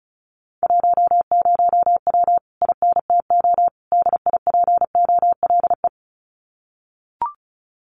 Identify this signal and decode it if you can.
Morse code, heard as a keyed tone, about 34 wpm, 700 Hz. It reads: 10W INTO DIPOLE
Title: Common QSO Phrases Encore 35wpm